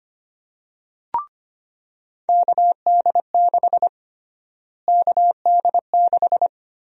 Morse code, heard as a keyed tone, about 25 wpm, 700 Hz. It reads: KD6 KD6
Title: USA Call Sign Prefixes - Repeated 2x - 25wpm